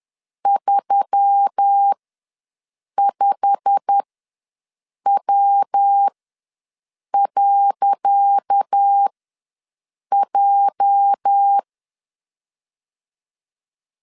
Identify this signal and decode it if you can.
Morse code, heard as a keyed tone, about 10 wpm, spaced as for 6 wpm, 800 Hz. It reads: 35W.J
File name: LEZIONI / 10 SIGNS / 10C DOT ws 10wpm 4213